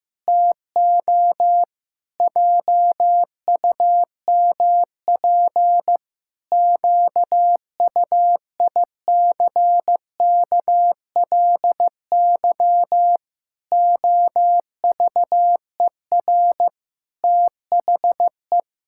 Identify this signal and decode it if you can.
Morse code, heard as a keyed tone, about 16 wpm, 700 Hz. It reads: TO JUMP QUICKLY OVER THE
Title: Pangram Challenge v2 - 15wpm